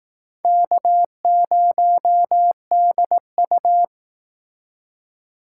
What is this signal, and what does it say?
Morse code, heard as a keyed tone, about 18 wpm, 700 Hz. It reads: K0DU
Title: Speed Racing - USA 1x2 Call Signs 15wpm